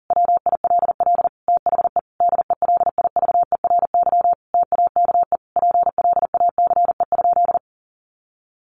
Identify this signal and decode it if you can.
Morse code, heard as a keyed tone, about 40 wpm, 700 Hz. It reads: WILL THE DELIVERY TAKE PLACE?
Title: Sentences from Top 1500 Words 40wpm